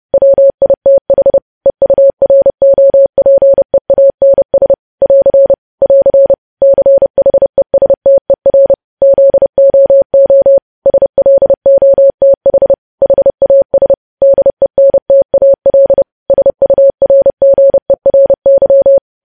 Morse code, heard as a keyed tone, about 30 wpm, 550 Hz. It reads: WITH EUROPEANS <AR> <AR> CHESTER ZOO SLOTH HAS DENTAL SURGERY
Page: news headlines in morse code at 30 wpm